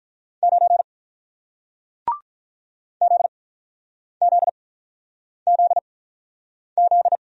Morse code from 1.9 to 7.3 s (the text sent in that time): ZZZZ